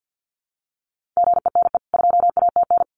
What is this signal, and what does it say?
Morse code, heard as a keyed tone, about 50 wpm, 700 Hz. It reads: GENE 2ATN
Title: CWT Exchange 50wpm